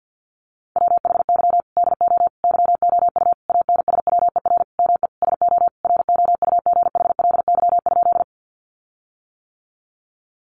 Morse code, heard as a keyed tone, about 50 wpm, 700 Hz. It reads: WHY DO YOU ANSWER ME SO ROUGHLY?
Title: Sentences from Top 1500 Words 50wpm